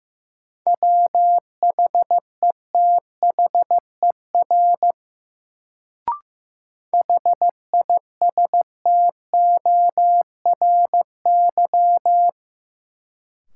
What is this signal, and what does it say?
Morse code, heard as a keyed tone, about 15 wpm, 700 Hz. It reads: WHETHER HISTORY
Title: Sets of 3 Words - New Word Drill - 15wpm